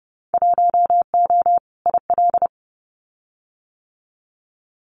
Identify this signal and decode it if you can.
Morse code, heard as a keyed tone, about 30 wpm, 700 Hz. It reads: 1O IL
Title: Winter Field Day Exchanges 30wpm